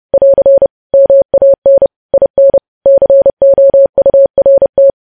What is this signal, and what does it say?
Morse code, heard as a keyed tone, about 30 wpm, 550 Hz. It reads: <AR> MAN IN COURT
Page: news headlines in morse code at 30 wpm